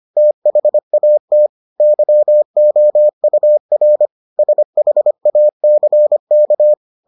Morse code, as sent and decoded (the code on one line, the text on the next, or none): - .... .- - / -.-- --- ..- .-. / ... .... .- -.-. -.-
THAT YOUR SHACK